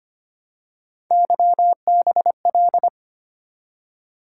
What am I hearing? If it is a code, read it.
Morse code, heard as a keyed tone, about 25 wpm, 700 Hz. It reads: YBL
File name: Three Letter-Number Combinations - 25wpm